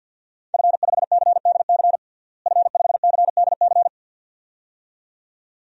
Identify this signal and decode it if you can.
Morse code, heard as a keyed tone, about 52 wpm, 700 Hz. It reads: F5CDC F5CDC
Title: DXCC-100 Call Signs - 5 Characters - Repeated 2x - 50wpm